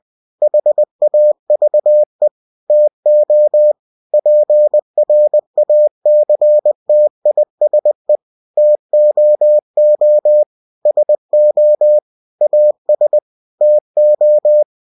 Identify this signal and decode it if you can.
Morse code, heard as a keyed tone, about 20 wpm, 600 Hz. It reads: HAVE TO PRACTISE TOO SO AS TO